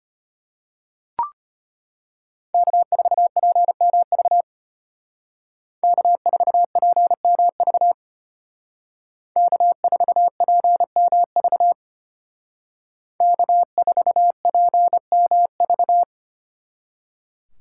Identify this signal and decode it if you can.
Morse code, heard as a keyed tone, about 30 wpm, 700 Hz. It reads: K4PMV K4PMV K4PMV K4PMV
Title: Speed Racing - USA 1x3 Call Signs 25wpm